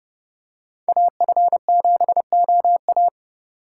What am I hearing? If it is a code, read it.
Morse code, heard as a keyed tone, about 28 wpm, 700 Hz. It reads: AF7OA